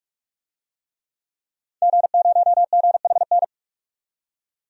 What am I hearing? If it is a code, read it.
Morse code, heard as a keyed tone, about 41 wpm, 700 Hz. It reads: G0GHN
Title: DXCC-100 Call Signs - 5 Characters - 45wpm